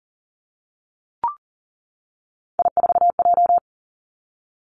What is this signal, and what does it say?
Morse code, heard as a keyed tone, about 38 wpm, 700 Hz. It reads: I4J
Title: Three Challenging Letter-Number Combinations 40wpm